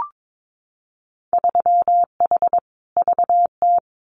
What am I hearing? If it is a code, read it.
Morse code, heard as a keyed tone, about 22 wpm, 700 Hz. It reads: 3H VT
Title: Winter Field Day Exchanges Encore 22wpm